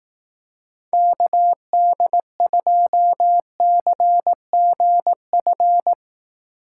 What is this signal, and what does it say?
Morse code, heard as a keyed tone, about 18 wpm, 700 Hz. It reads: KD2CGF
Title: Speed Racing - USA 2x3 Call Signs 15wpm